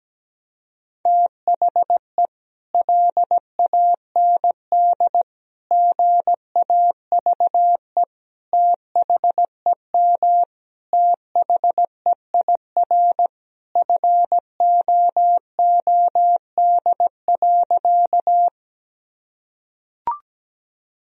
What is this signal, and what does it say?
Morse code, heard as a keyed tone, about 17 wpm, 700 Hz. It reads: THE LAND GAVE THEM THEIR FOOD.
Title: Sentences from Top 1000 Words - Review Part 1 - 17wpm